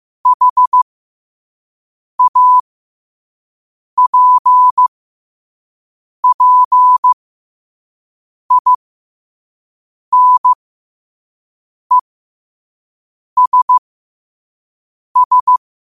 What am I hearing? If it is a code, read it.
Morse code, heard as a keyed tone, about 15 wpm, spaced as for 5 wpm, 1000 Hz. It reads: HAPPINESS
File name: qotdfiles / 5wpm